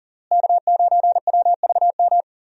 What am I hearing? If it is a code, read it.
Morse code, heard as a keyed tone, about 40 wpm, 700 Hz. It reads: K9WVM
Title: USA Call Signs - 5 Characters - Repeated 3x - 40wpm